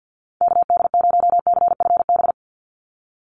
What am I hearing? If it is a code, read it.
Morse code, heard as a keyed tone, about 50 wpm, 700 Hz. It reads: KD0CFB
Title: USA Call Signs 50wpm